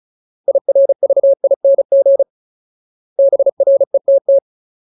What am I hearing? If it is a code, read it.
Morse code, heard as a keyed tone, about 34 wpm, 550 Hz. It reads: IRVING BRETT
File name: Ultimate Word List / 35WPM Ultimate